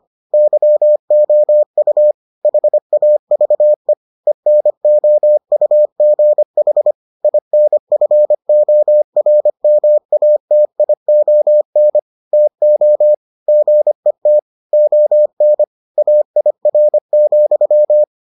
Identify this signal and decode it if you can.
Morse code, heard as a keyed tone, about 25 wpm, 600 Hz. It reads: YOU HAVE ENOUGH INFORMATION TO GET ON AIR,